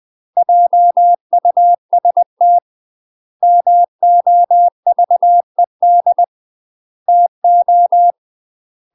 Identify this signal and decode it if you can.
Morse code, heard as a keyed tone, about 20 wpm, 700 Hz. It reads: JUST MOVED TO
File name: UK MASTER QSOs TRAINING / 20WPM / UK MASTER QSOs 0013